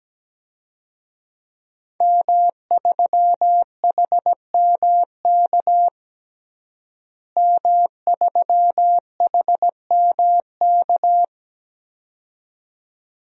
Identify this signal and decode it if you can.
Morse code, heard as a keyed tone, about 17 wpm, 700 Hz. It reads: M3HMK M3HMK
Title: DXCC-100 Call Signs - 5 Characters - Repeated 2x - 17wpm